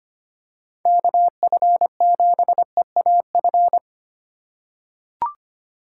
Morse code, heard as a keyed tone, about 25 wpm, 700 Hz. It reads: KF7EAF
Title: USA Call Signs - 6 Characters - 4x Character Spacing - 25wpm